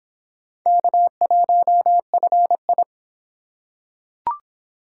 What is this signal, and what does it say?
Morse code, heard as a keyed tone, about 22 wpm, 700 Hz. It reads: K1FI